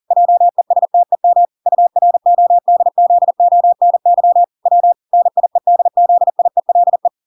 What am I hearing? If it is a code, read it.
Morse code, heard as a keyed tone, about 40 wpm, 700 Hz. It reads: JESTEM URODZONY W NIEDZIELE